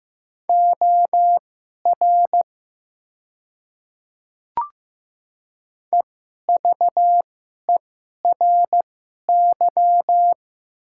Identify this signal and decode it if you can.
Morse code, heard as a keyed tone, about 15 wpm, spaced as for 11 wpm, 700 Hz. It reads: OR EVERY
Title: 2x Character Spacing - Top 200 Words - 15wpm